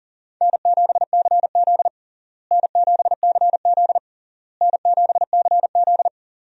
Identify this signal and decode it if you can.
Morse code, heard as a keyed tone, about 40 wpm, 700 Hz. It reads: N7CZ N7CZ N7CZ
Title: USA Call Signs - 4 Characters - Repeated 3x - 40wpm